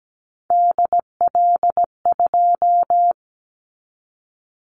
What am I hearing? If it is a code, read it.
Morse code, heard as a keyed tone, about 17 wpm, 700 Hz. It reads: DL2
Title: Callsign Prefixes for Top 25 DXCC - 17wpm